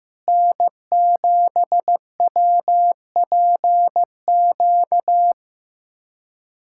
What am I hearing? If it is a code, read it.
Morse code, heard as a keyed tone, about 15 wpm, 700 Hz. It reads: N7WPQ
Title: USA Callsign DE Callsign 15wpm